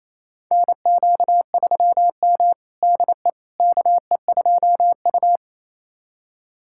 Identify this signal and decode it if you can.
Morse code, heard as a keyed tone, about 28 wpm, 700 Hz. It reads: NQ3M DE KE2U
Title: USA Callsign DE Callsign - 4 Characters - 28wpm